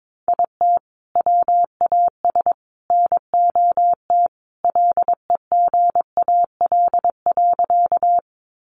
Morse code, heard as a keyed tone, about 22 wpm, 700 Hz. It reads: IT WAS NOT LEGAL.
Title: Sentences from Top 1200 Words 22wpm